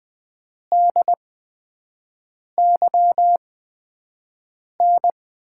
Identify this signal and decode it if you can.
Morse code, heard as a keyed tone, about 20 wpm, spaced as for 5 wpm, 700 Hz. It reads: DYN
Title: USA Call Signs - 6 Characters - 8x Character Spacing - 20wpm